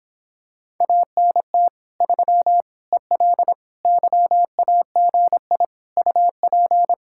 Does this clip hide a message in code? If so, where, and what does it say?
Yes: Morse code from 0.8 to 6.9 s, heard as a keyed tone, about 26 wpm, 700 Hz. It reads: ANT 3 EL YAGI UP